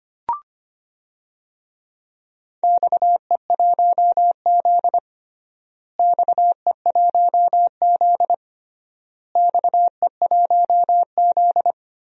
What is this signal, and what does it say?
Morse code, heard as a keyed tone, about 25 wpm, 700 Hz. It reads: XE1Z XE1Z XE1Z
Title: DXCC-100 Call Signs - 4 Characters - Repeated 3x - 25wpm